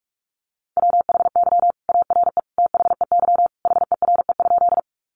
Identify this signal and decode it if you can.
Morse code, heard as a keyed tone, about 44 wpm, 700 Hz. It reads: WHY ARE THEY HERE?
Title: Sentences from Top 200 Words 45wpm